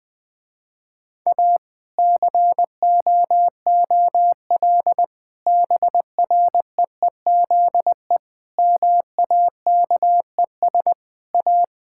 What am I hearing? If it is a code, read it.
Morse code, heard as a keyed tone, about 21 wpm, 700 Hz. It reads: A COOL BREEZE MAKES A
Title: Difficult Sentences with Hints - 20wpm